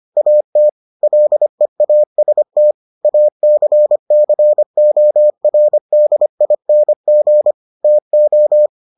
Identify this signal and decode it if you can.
Morse code, heard as a keyed tone, about 25 wpm, 600 Hz. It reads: AT LEAST ACCORDING TO